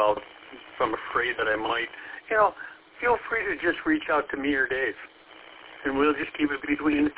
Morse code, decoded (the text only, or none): none